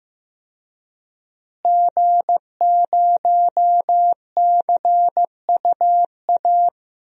Morse code, heard as a keyed tone, about 15 wpm, 700 Hz. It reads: G0CUA
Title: DXCC-100 Call Signs - 5 Characters - Repeated 3x - 15wpm